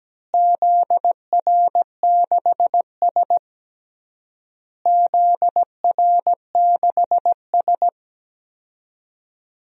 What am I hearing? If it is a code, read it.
Morse code, heard as a keyed tone, about 17 wpm, 700 Hz. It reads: ZR6S ZR6S
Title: DXCC-100 Call Signs - 4 Characters - Repeated 3x - 17wpm